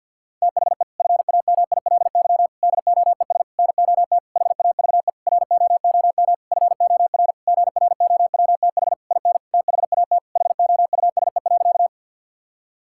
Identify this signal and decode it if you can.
Morse code, heard as a keyed tone, about 52 wpm, 690 Hz. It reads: THE FAMILY DOES NOT HAVE ROOM FOR GROWTH IN THAT HOUSE.
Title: Sentences from Top 1000 Words - Review Part 5 - 50wpm